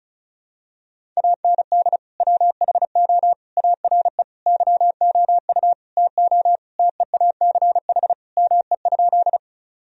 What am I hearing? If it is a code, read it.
Morse code, heard as a keyed tone, about 36 wpm, 700 Hz. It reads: AND WHO ARE YOU TO TEACH ME?
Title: Sentences from Top 1000 Words - Review Part 2 - 35wpm